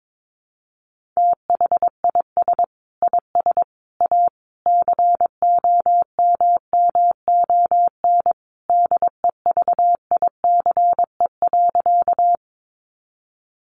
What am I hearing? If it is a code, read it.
Morse code, heard as a keyed tone, about 22 wpm, 700 Hz. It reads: THIS IS A COMMON DEVICE.
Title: Sentences from Top 1500 Words 22wpm